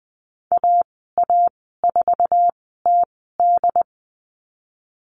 Morse code, heard as a keyed tone, about 20 wpm, spaced as for 15 wpm, 700 Hz. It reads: AA4TD
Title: 147 Lesson 24g - Letter y - Call Signs 20wpm